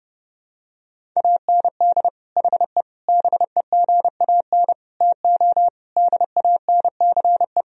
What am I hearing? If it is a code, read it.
Morse code, heard as a keyed tone, about 32 wpm, 700 Hz. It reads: AND HE BEGAN TO DANCE
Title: Sentences from Top 1000 Words - Review Part 4 - 30wpm